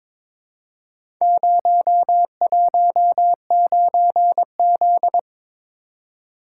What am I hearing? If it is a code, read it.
Morse code, heard as a keyed tone, about 23 wpm, 700 Hz. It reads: 019Z
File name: practice / 22wpm / Four Letter-Number Combinations - 22wpm